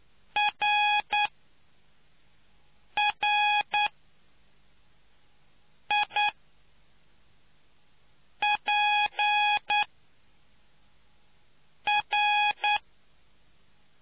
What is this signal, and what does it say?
Morse code, heard as a keyed tone, about 9 wpm, spaced as for 4 wpm, 830 Hz. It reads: RRIPR